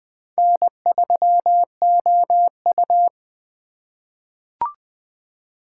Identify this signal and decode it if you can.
Morse code, heard as a keyed tone, about 21 wpm, 700 Hz. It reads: N3OU
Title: USA Call Signs - 4 Characters - 8x Character Spacing - 20wpm